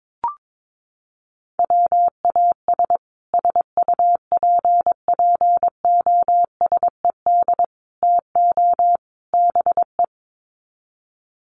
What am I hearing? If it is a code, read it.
Morse code, heard as a keyed tone, about 22 wpm, 700 Hz. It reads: WAS SUPPOSED TO BE
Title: Sets of 4 Words 22wpm